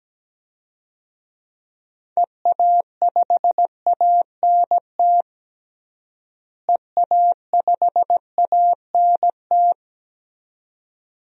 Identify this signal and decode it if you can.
Morse code, heard as a keyed tone, about 17 wpm, 700 Hz. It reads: EA5ANT EA5ANT
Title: DXCC-100 Call Signs - 6 Characters - Repeated 2x - 17wpm